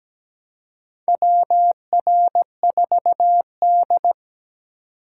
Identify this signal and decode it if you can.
Morse code, heard as a keyed tone, about 17 wpm, 700 Hz. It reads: WR4D
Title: USA Call Signs - 4 Characters - 4x Character Spacing - 17wpm